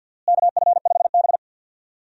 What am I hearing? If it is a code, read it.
Morse code, heard as a keyed tone, about 48 wpm, 700 Hz. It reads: KF5B
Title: USA Call Signs - 4 Characters - Repeated 2x - 50wpm